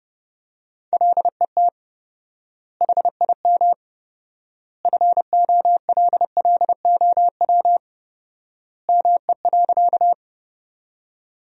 Morse code, heard as a keyed tone, about 32 wpm, 700 Hz. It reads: LET HIM FOLLOW ME.
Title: 4x Word-Spacing - Sentences from Top 300 Words - 30wpm